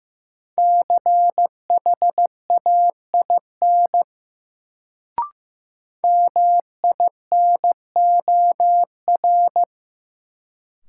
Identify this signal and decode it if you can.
Morse code, heard as a keyed tone, about 15 wpm, 700 Hz. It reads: CHAIN MINOR
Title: Sentences from Top 1500 Words - New Word Drill Alt - 15wpm